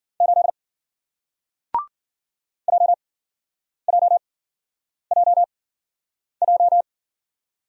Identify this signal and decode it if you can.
Morse code, heard as a keyed tone, about 54 wpm, spaced as for 9 wpm, 700 Hz. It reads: 8 JJJJ